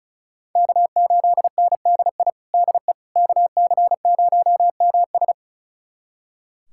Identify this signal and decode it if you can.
Morse code, heard as a keyed tone, about 34 wpm, 700 Hz. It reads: K8NDI DE KC0MS
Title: USA Callsign DE Callsign - 5 Characters - 35wpm